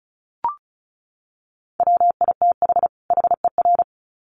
Morse code, heard as a keyed tone, about 34 wpm, 700 Hz. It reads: WITH HER
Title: Sets of 2 Words 35wpm